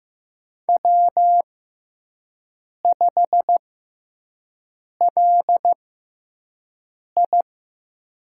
Morse code, heard as a keyed tone, about 15 wpm, spaced as for 5 wpm, 700 Hz. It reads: W5LI
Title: USA Call Signs - 6x Character Spacing - 15wpm